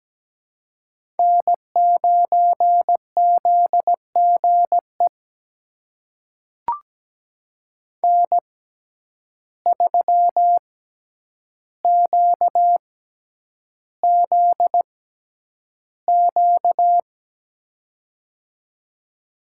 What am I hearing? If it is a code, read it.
Morse code, heard as a keyed tone, about 17 wpm, spaced as for 9 wpm, 700 Hz. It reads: N9ZGE N3QZQ
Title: USA Call Signs - 5 Characters - 6x Character Spacing - 17wpm